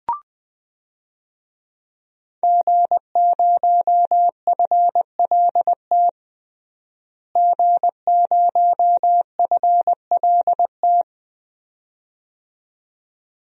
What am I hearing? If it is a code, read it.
Morse code, heard as a keyed tone, about 20 wpm, 700 Hz. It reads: G0FLT G0FLT
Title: DXCC-100 Call Signs - 5 Characters - Repeated 2x - 20wpm